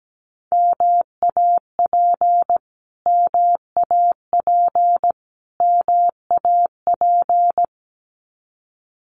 Morse code, heard as a keyed tone, about 17 wpm, 700 Hz. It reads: MAP MAP MAP
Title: Top 500 Words Repeated 17wpm